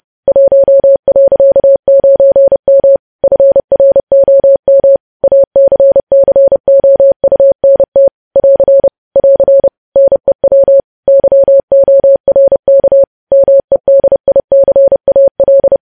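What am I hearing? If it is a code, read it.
Morse code, heard as a keyed tone, about 30 wpm, 550 Hz. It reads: 1.9M FROM ACCOUNT <AR> <AR> NEW YORK MEDICAL
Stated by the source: News headlines sent in Morse code at 30.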